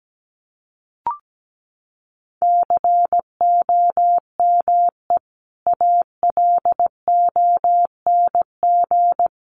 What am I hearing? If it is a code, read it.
Morse code, heard as a keyed tone, about 17 wpm, 700 Hz. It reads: COME ALONG
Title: Sentences from Top 700 Words 17wpm